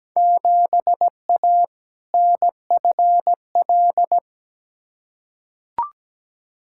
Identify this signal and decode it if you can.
Morse code, heard as a keyed tone, about 17 wpm, 700 Hz. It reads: 7A NFL
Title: Field Day Exchanges Encore - 17wpm